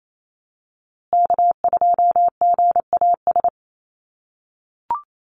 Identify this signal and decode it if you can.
Morse code, heard as a keyed tone, about 28 wpm, 700 Hz. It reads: K2GAS
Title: USA Call Signs Repeated 2x 28wpm